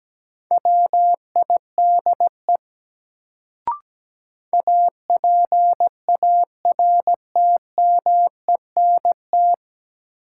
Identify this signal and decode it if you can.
Morse code, heard as a keyed tone, about 17 wpm, 700 Hz. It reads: WIDE APARTMENT
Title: Sentences from Top 900 Words - New Word Drill - 17wpm